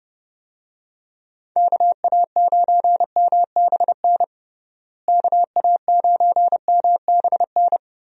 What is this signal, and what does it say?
Morse code, heard as a keyed tone, about 29 wpm, 700 Hz. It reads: KA9MBN KA9MBN
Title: USA Call Signs - 6 Characters - Repeated 2x - 30wpm